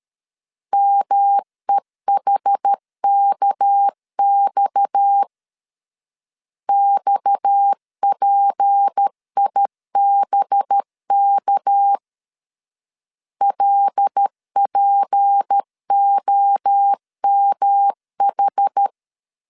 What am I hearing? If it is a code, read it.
Morse code, heard as a keyed tone, about 13 wpm, 800 Hz. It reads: MEHKX XPIBK LPOMH
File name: LEZIONI / 6 PKX0 / 6 PKX 12wpm 2024